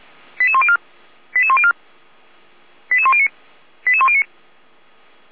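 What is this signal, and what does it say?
Five-tone selective call (ZVEI), heard as quick runs of five tones, tones of about 70 ms. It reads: 88104 88104 88109 88109
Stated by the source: ZVEI five-tone selective call received on an SDR, recorded by the feuerwehr-orlamuende.org fire brigade server.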